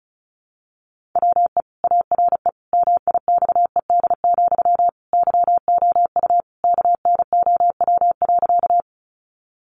Morse code, heard as a keyed tone, about 34 wpm, 700 Hz. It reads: WE ARE MIXED, YOU KNOW.
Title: Sentences from Top 1500 Words 35wpm